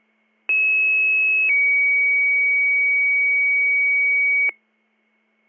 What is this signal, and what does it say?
Two-tone page, heard as a one-second tone, then a three-second tone, about 2600 then 2250 Hz, with noise. It is heard from 0.5 to 4.5 s.